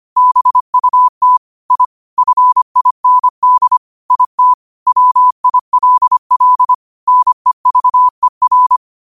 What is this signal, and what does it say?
Morse code, heard as a keyed tone, about 25 wpm, 1000 Hz. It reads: BUT I FIND IT WILL NEVER
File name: qotdfiles / 25wpm